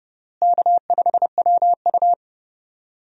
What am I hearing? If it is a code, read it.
Morse code, heard as a keyed tone, about 30 wpm, 700 Hz. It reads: K5WU
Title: USA Call Signs - 4 Characters - Repeated 3x - 30wpm